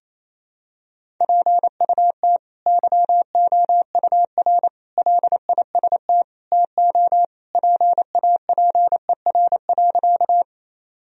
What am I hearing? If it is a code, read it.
Morse code, heard as a keyed tone, about 28 wpm, 700 Hz. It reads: PUT YOUR LIST TO PAPER.
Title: Sentences from Top 1000 Words - Review Part 2 - 28wpm